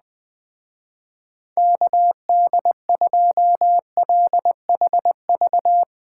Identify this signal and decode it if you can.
Morse code, heard as a keyed tone, about 20 wpm, 700 Hz. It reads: KD2LHV
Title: Speed Racing - USA 2x3 Call Signs 17wpm